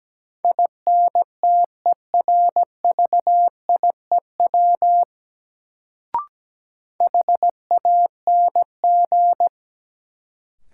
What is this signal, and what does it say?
Morse code, heard as a keyed tone, about 17 wpm, 700 Hz. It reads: INTERVIEW HANG
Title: Sentences from Top 1600 Words - New Word Drill - 17wpm